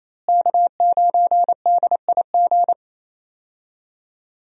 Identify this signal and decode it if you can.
Morse code, heard as a keyed tone, about 27 wpm, 700 Hz. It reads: K9DIG
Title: USA Call Signs - 5 Characters - Repeated 3x - 28wpm